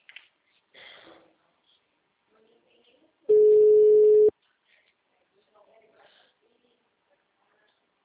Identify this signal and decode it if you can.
Ringback tone (Europe, 425 Hz), with speech.